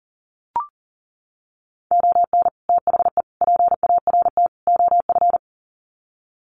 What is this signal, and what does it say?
Morse code, heard as a keyed tone, about 40 wpm, 700 Hz. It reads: ON THE PART OF
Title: Sets of 4 Words 40wpm